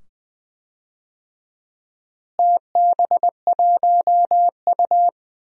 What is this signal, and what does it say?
Morse code, heard as a keyed tone, about 21 wpm, 700 Hz. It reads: TB1U
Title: DXCC-100 Call Signs - 4 Characters - Repeated 2x - 20wpm